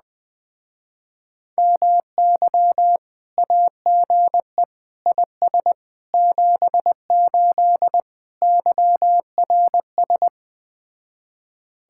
Morse code, heard as a keyed tone, about 20 wpm, 700 Hz. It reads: MY AGE IS 78 YRS
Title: Speed-Racing Common QSO Phrases Encore - 17wpm